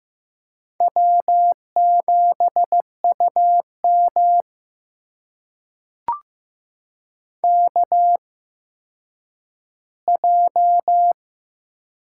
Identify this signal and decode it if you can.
Morse code, heard as a keyed tone, about 15 wpm, spaced as for 9 wpm, 700 Hz. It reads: W7UM KJ
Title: USA Call Signs - 8x Character Spacing - 15wpm